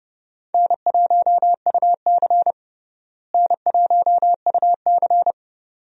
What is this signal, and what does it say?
Morse code, heard as a keyed tone, about 30 wpm, 700 Hz. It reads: N1UC N1UC
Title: USA Call Signs - 4 Characters - Repeated 2x - 30wpm